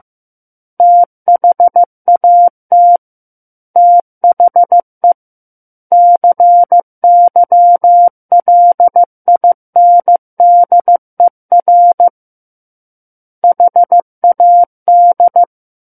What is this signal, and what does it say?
Morse code, heard as a keyed tone, about 15 wpm, 700 Hz. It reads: THAT THE CYLINDER HAD
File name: Book One / 06-Chapter 6 - 15 WPM